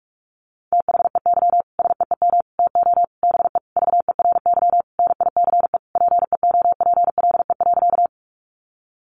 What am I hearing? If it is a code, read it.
Morse code, heard as a keyed tone, about 44 wpm, 700 Hz. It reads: THEY SEEM TO BE VERY NICE PEOPLE.
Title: Sentences from Top 1100 Words 45wpm